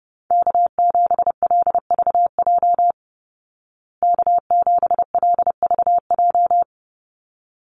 Morse code, heard as a keyed tone, about 30 wpm, 700 Hz. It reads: K7LVJ K7LVJ
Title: Call Signs Repeated 3x - 30wpm